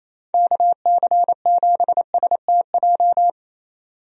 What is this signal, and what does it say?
Morse code, heard as a keyed tone, about 28 wpm, 700 Hz. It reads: KC7STJ
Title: USA Call Signs - 6 Characters - 28wpm